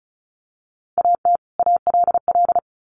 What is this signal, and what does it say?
Morse code, heard as a keyed tone, about 34 wpm, 700 Hz. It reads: AT ALL
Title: Sets of 2 Words 35wpm